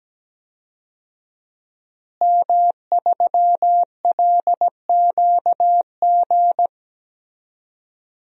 Morse code, heard as a keyed tone, about 17 wpm, 700 Hz. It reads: M3LQG
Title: DXCC-100 Call Signs - 5 Characters - Repeated 2x - 17wpm